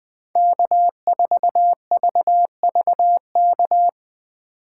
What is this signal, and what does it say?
Morse code, heard as a keyed tone, about 20 wpm, 700 Hz. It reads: K4VVK
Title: USA Call Signs - 5 Characters - 3x Character Spacing - 20wpm